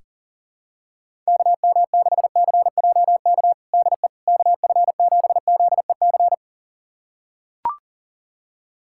Morse code, heard as a keyed tone, about 40 wpm, 700 Hz. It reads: KM6CJK DE KF7ZEC
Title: USA Callsign DE Callsign - 6 Characters - 40wpm